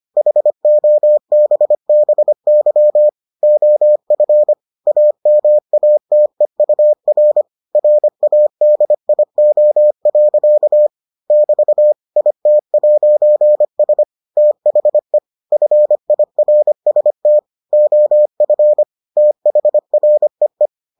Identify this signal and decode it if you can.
Morse code, heard as a keyed tone, about 25 wpm, 600 Hz. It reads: HOBBY OF AMATEUR RADIO. <BT> IT'S THE FIRST OF THREE